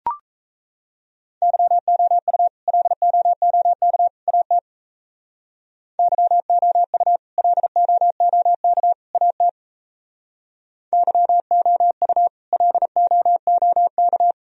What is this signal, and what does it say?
Morse code, heard as a keyed tone, about 37 wpm, 700 Hz. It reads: YOU LOOK AT YOU LOOK AT YOU LOOK